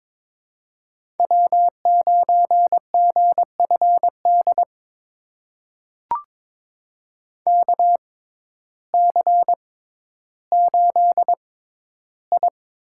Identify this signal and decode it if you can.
Morse code, heard as a keyed tone, about 22 wpm, spaced as for 12 wpm, 700 Hz. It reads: W9GFD KC8I
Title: USA Call Signs - 5 Characters - 6x Character Spacing - 22wpm